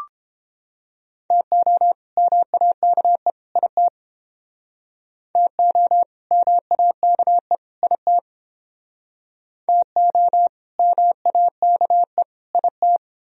Morse code, heard as a keyed tone, about 30 wpm, 700 Hz. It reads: TO MAKE IT TO MAKE IT TO MAKE IT